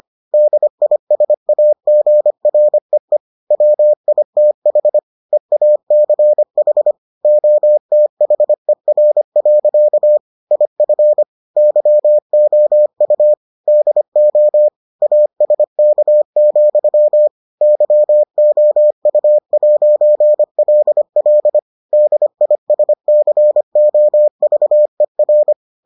Morse code, heard as a keyed tone, about 25 wpm, 600 Hz. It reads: DISAGREE WITH EACH OTHER. IF YOU DO ASK, YOU'LL DISCOVER